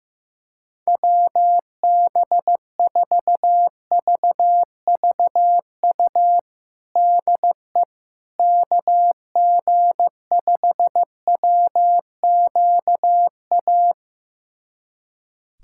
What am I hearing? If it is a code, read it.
Morse code, heard as a keyed tone, about 15 wpm, 700 Hz. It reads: WB4VVU DE KG5WQA
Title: USA Callsign DE Callsign - 6 Characters - 15wpm